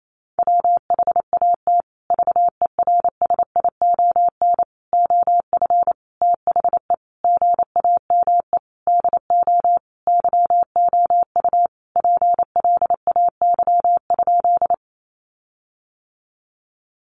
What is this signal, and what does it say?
Morse code, heard as a keyed tone, about 28 wpm, 700 Hz. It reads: WHAT VERSION OF THE GAME DO YOU PLAY?
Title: Sentences from Top 1400 Words 28wpm